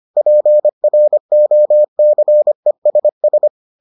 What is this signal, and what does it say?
Morse code, heard as a keyed tone, about 25 wpm, 600 Hz. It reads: PROCESS